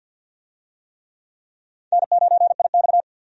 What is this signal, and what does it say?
Morse code, heard as a keyed tone, about 52 wpm, 700 Hz. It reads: N9IX
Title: USA Call Signs - 4 Characters - 50wpm